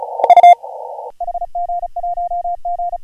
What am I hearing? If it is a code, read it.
Morse code, heard as a keyed tone, about 41 wpm, 690 Hz. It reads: U HG1G
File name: AUDIO2019 / HG1G CW